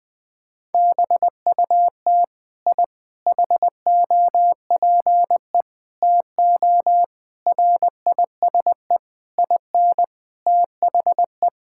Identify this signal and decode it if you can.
Morse code, heard as a keyed tone, about 20 wpm, 700 Hz. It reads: BUT I HOPE TO RISE IN THE
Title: Sentences from Top 1000 Words - Review Part 6 - 20wpm